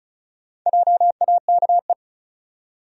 Morse code, heard as a keyed tone, about 36 wpm, 700 Hz. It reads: JAKE
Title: USA Common Names - Max 4 Letters - 35wpm